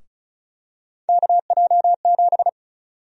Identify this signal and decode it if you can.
Morse code, heard as a keyed tone, about 34 wpm, 700 Hz. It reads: KJ7
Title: USA Call Sign Prefixes - 35wpm